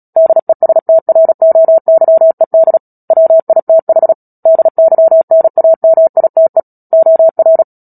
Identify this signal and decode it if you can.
Morse code, heard as a keyed tone, about 37 wpm, 650 Hz. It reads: DESTROYED WITH DYNAMITE OR
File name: Book Two / 10-Chapter 10 - Epilogue - 36 WPM